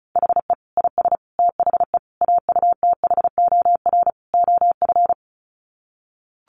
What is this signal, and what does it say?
Morse code, heard as a keyed tone, about 34 wpm, 700 Hz. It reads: HE IS THE AUTHOR OF
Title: Sets of 5 Words Encore2 35wpm